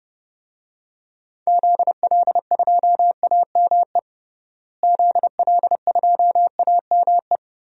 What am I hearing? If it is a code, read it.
Morse code, heard as a keyed tone, about 29 wpm, 700 Hz. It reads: ZL2AME ZL2AME
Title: DXCC-100 Call Signs - 6 Characters - Repeated 2x - 30wpm